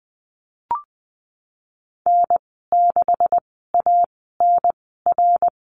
Morse code, heard as a keyed tone, about 20 wpm, spaced as for 15 wpm, 700 Hz. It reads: N6ANR
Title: 147 Lesson 24g - Letter y - Call Signs 20wpm